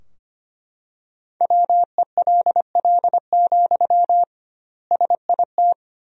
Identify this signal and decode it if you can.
Morse code, heard as a keyed tone, about 25 wpm, 700 Hz. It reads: WELL, SIT
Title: 2x Word-Spacing - Sentences from Top 500 Words - 25wpm